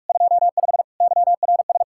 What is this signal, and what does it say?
Morse code, heard as a keyed tone, about 64 wpm, spaced as for 54 wpm, 700 Hz. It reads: 25 YRS
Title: Speed-Racing Common QSO Phrases - 45wpm